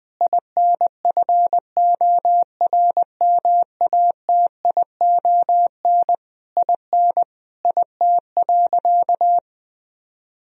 Morse code, heard as a keyed tone, about 20 wpm, 700 Hz. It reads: INFORMATION IN IT.
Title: Sentences from Top 1000 Words - Review Part 4 - 20wpm